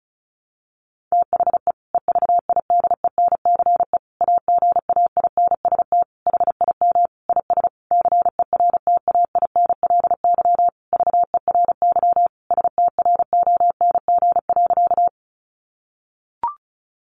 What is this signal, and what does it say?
Morse code, heard as a keyed tone, about 34 wpm, 700 Hz. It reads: THE EVIDENCE AGAINST HIM IS CERTAINLY VERY STRONG.
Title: Sentences from Top 1300 Words 35wpm